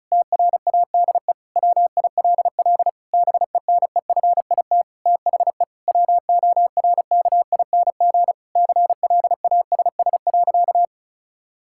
Morse code, heard as a keyed tone, about 36 wpm, 700 Hz. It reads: TRADE WILL BENEFIT THE WORKING CLASS.
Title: Sentences from Top 1000 Words - Review Part 6 - 35wpm